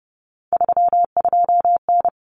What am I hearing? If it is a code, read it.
Morse code, heard as a keyed tone, about 30 wpm, 700 Hz. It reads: 32N
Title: Common Cut Numbers - 3-Digit 30wpm